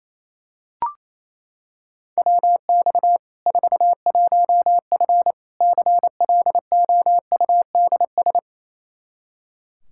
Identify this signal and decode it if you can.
Morse code, heard as a keyed tone, about 28 wpm, 700 Hz. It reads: WX 41F CLOUDS
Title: Practice QSO - Exchange 2 - Info - 28wpm